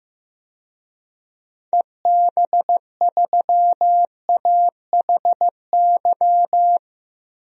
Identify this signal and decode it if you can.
Morse code, heard as a keyed tone, about 15 wpm, 700 Hz. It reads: EB3AHY
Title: DXCC-100 Call Signs - 6 Characters - 15wpm